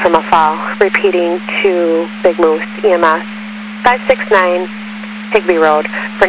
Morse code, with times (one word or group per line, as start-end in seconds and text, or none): none